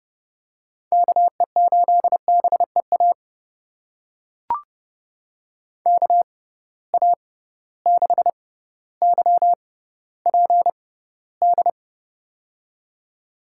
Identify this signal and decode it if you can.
Morse code, heard as a keyed tone, about 32 wpm, spaced as for 16 wpm, 700 Hz. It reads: KE8BEA KA6YPD
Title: USA Call Signs - 6 Characters - 6x Character Spacing - 30wpm